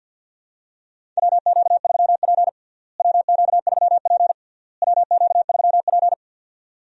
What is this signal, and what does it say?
Morse code, heard as a keyed tone, about 49 wpm, 700 Hz. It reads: WQ3P WQ3P WQ3P
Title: USA Call Signs - 4 Characters - Repeated 3x - 50wpm